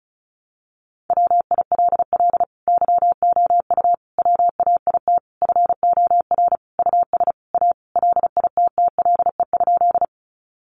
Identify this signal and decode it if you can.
Morse code, heard as a keyed tone, about 34 wpm, 700 Hz. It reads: WILL YOU WAIT FOR US A LITTLE?
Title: Sentences from Top 900 Words 35wpm